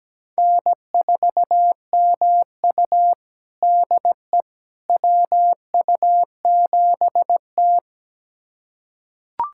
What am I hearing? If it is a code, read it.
Morse code, heard as a keyed tone, about 17 wpm, 700 Hz. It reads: N4MU DE WU7T
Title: USA Callsign DE Callsign - 4 Characters - 17wpm